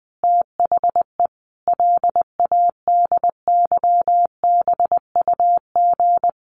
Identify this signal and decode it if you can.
Morse code, heard as a keyed tone, about 20 wpm, 700 Hz. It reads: THE LADYBUG
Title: Additional Easy Phrases 20wpm